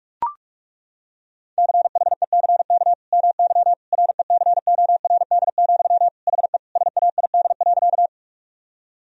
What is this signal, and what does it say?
Morse code, heard as a keyed tone, about 41 wpm, 700 Hz. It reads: CHECK MY RECORD, HE SAID.
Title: Sentences from Top 1000 Words - Review Part 3 - 45wpm